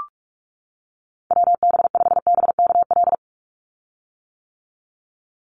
Morse code, heard as a keyed tone, about 44 wpm, 700 Hz. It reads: WB5BKL
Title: Call Signs - 45wpm